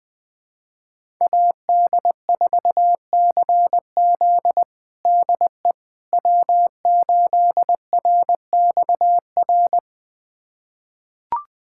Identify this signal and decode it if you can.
Morse code, heard as a keyed tone, about 20 wpm, 700 Hz. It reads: AD4CZ DE W8RXR
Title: USA Callsign DE Callsign - 5 Characters - 20wpm